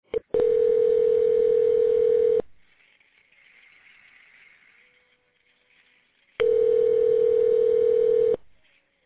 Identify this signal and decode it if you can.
Ringback tone (North America), 440 + 480 Hz.